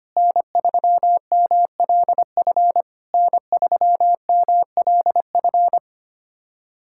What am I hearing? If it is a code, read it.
Morse code, heard as a keyed tone, about 25 wpm, 690 Hz. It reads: N3MLF N3MLF
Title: SOTA QSO Practice 25wpm